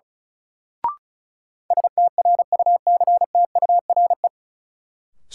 Morse code, heard as a keyed tone, about 36 wpm, 700 Hz. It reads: STRUCTURE
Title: Sentences from Top 1000 Words - Review Part 6 - New Word Drill - 35wpm